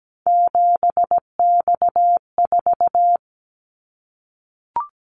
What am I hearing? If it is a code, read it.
Morse code, heard as a keyed tone, about 17 wpm, 700 Hz. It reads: 7X4
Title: Callsign Prefixes for Top 100 DXCC - 17wpm